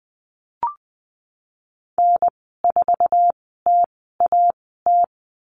Morse code, heard as a keyed tone, about 20 wpm, spaced as for 15 wpm, 700 Hz. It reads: N4TAT
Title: 048 Lesson 9g - Number 4 - Call Signs 20wpm